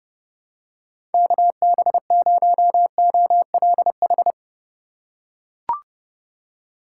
Morse code, heard as a keyed tone, about 30 wpm, 700 Hz. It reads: KB0OLH
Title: USA Call Signs - 6 Characters - 3x Character Spacing - 30wpm